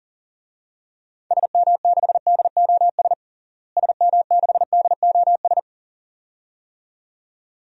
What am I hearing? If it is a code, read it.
Morse code, heard as a keyed tone, about 40 wpm, 700 Hz. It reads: SM6DOS SM6DOS
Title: DXCC-100 Call Signs - 6 Characters - Repeated 2x - 40wpm